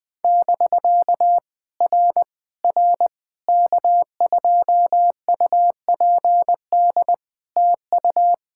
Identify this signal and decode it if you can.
Morse code, heard as a keyed tone, about 20 wpm, 700 Hz. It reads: <BK> R R K2UPD TU
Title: POTA QSO Practice - 20wpm